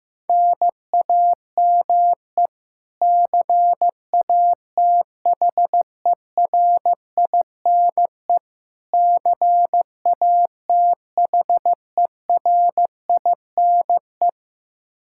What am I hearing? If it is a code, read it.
Morse code, heard as a keyed tone, about 15 wpm, 700 Hz. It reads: NAME CATHERINE CATHERINE
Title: Practice QSO - Exchange 1 - Beginning - 15wpm